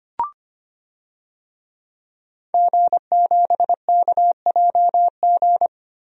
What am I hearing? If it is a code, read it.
Morse code, heard as a keyed tone, about 25 wpm, 700 Hz. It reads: G7KJG
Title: DXCC-100 Call Signs - 5 Characters - Repeated 3x - 25wpm